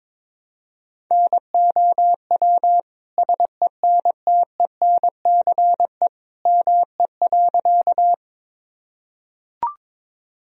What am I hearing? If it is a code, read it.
Morse code, heard as a keyed tone, about 23 wpm, 700 Hz. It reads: NOW SENTENCE ME.
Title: Sentences from Top 1000 Words - Review Part 1 - 22wpm